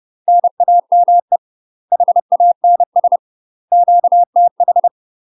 Morse code, heard as a keyed tone, about 30 wpm, 700 Hz. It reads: NAME HANS QTH
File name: UK MASTER QSOs TRAINING / 30WPM / UK MASTER QSOs 0027